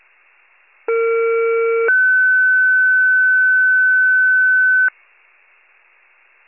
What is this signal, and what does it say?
Two-tone page, heard as a one-second tone, then a three-second tone, about 460 then 1500 Hz. It is heard from 0.9 to 4.9 s.
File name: pager / 21 FIRE / wcalert